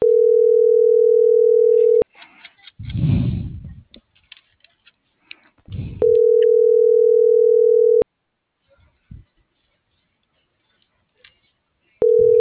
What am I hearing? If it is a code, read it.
Ringback tone (North America), 440 + 480 Hz.